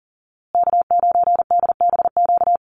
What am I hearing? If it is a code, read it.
Morse code, heard as a keyed tone, about 40 wpm, 700 Hz. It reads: K9DBQ
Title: USA Call Signs Repeated 3x 40wpm